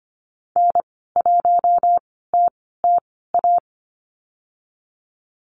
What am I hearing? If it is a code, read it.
Morse code, heard as a keyed tone, about 25 wpm, spaced as for 16 wpm, 700 Hz. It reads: N1TTA
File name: Lesson 15g - Number 5 - Call Signs 25wpm